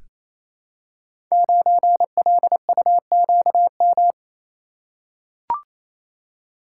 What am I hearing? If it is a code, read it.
Morse code, heard as a keyed tone, about 28 wpm, 700 Hz. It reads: 9LUQM
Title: Code Groups - 5 Letter:Numbers - 28wpm